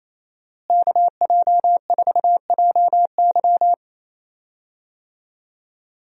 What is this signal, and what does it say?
Morse code, heard as a keyed tone, about 28 wpm, 700 Hz. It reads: KJ4JY
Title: USA Call Signs - 5 Characters - Repeated 2x - 28wpm